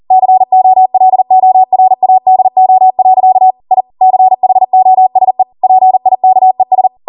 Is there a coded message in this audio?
Yes: Morse code from 0.1 to 6.9 s, heard as a keyed tone, about 40 wpm, 750 Hz. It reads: COLORADO. I CHOSE PIKES